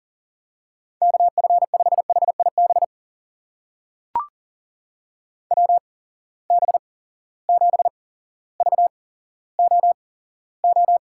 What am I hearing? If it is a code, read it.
Morse code, heard as a keyed tone, about 40 wpm, spaced as for 17 wpm, 700 Hz. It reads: KF5HIB WB7VOO
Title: USA Call Signs - 6 Characters - 8x Character Spacing - 40wpm